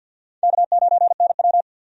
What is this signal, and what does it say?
Morse code, heard as a keyed tone, about 52 wpm, 700 Hz. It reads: K9NW
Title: USA Call Signs - 4 Characters - Repeated 2x - 50wpm